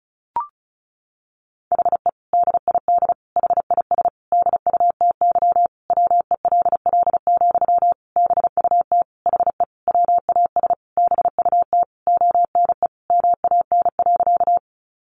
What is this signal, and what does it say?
Morse code, heard as a keyed tone, about 34 wpm, 700 Hz. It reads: HE DID HIS DUTY WELL, BUT HE WAS BUT ONE MAN.
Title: Sentences from Top 1500 Words 35wpm